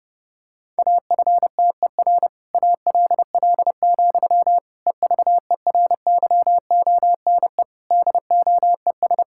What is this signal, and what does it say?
Morse code, heard as a keyed tone, about 30 wpm, 700 Hz. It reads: AFTER ALL, EVERYONE DOES
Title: Sentences from Top 1000 Words - Review Part 5 - 30wpm